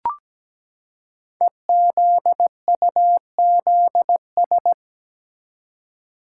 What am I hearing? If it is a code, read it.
Morse code, heard as a keyed tone, about 17 wpm, 700 Hz. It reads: EZUZS
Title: Code Groups - 5 Letters - 17wpm